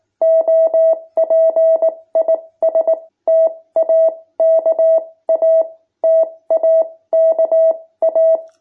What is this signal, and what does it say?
Morse code, heard as a keyed tone, about 19 wpm, spaced as for 16 wpm, 650 Hz. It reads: OP IS TAKA TAKA
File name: jh8cbh / 2021cw